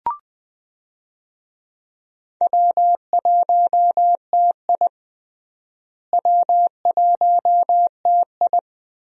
Morse code, heard as a keyed tone, about 21 wpm, 700 Hz. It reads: W1TI W1TI
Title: USA Call Signs - 4 Characters - Repeated 2x - 20wpm